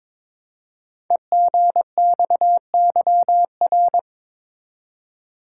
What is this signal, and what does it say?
Morse code, heard as a keyed tone, about 23 wpm, 700 Hz. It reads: EGXYR
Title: Code Groups - 5 Letters - 22wpm